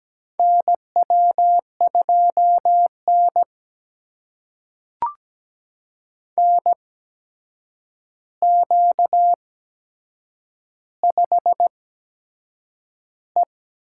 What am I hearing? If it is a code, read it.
Morse code, heard as a keyed tone, about 17 wpm, spaced as for 7 wpm, 700 Hz. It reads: NW2N NQ5E
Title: USA Call Signs - 4 Characters - 8x Character Spacing - 17wpm